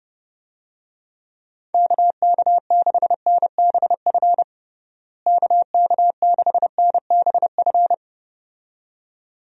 Morse code, heard as a keyed tone, about 30 wpm, 700 Hz. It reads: KK6NBF KK6NBF
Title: USA Call Signs - 6 Characters - Repeated 2x - 30wpm